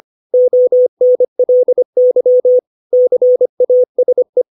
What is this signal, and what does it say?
Morse code, heard as a keyed tone, about 25 wpm, 500 Hz. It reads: ONLY CASE
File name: 632 words / 632 Words 25WPM